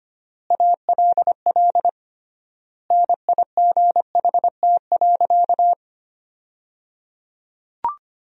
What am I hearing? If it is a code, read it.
Morse code, heard as a keyed tone, about 25 wpm, 700 Hz. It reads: ALL NIGHT.
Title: 3x Word-Spacing - Sentences from Top 500 Words - 25wpm